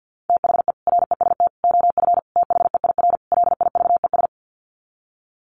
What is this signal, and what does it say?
Morse code, heard as a keyed tone, about 50 wpm, 700 Hz. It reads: THE REST OF THEIR LIVES
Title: Sets of 5 Words Encore 50wpm